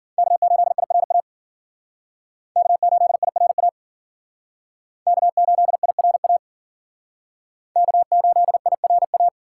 Morse code, heard as a keyed tone, about 53 wpm, 700 Hz. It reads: K8IRA K8IRA K8IRA K8IRA